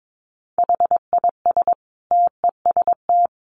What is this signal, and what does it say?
Morse code, heard as a keyed tone, about 22 wpm, 700 Hz. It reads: HIS TEST
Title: Additional Easy Phrases 22wpm